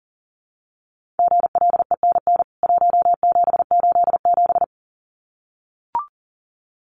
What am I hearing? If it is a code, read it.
Morse code, heard as a keyed tone, about 40 wpm, 700 Hz. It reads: GLENN 1787
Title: CWT Exchange 40wpm